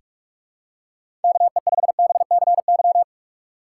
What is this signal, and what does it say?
Morse code, heard as a keyed tone, about 42 wpm, 700 Hz. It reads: KE5BCY
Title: K1USN SST QSO Practice - 45wpm